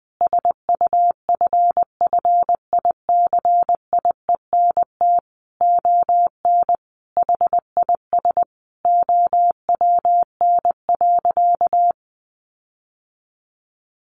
Morse code, heard as a keyed tone, about 20 wpm, 700 Hz. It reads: SUFFICIENT ON HIS OWN.
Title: Sentences from Top 1400 Words 20wpm